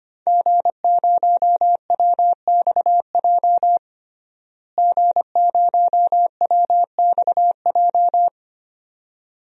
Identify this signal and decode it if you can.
Morse code, heard as a keyed tone, about 25 wpm, 700 Hz. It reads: G0WXJ G0WXJ
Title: DXCC-100 Call Signs - 5 Characters - Repeated 2x - 25wpm